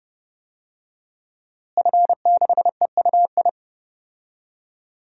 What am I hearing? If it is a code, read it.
Morse code, heard as a keyed tone, about 30 wpm, 700 Hz. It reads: F6EUI
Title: DXCC-100 Call Signs - 5 Characters - Repeated 2x - 30wpm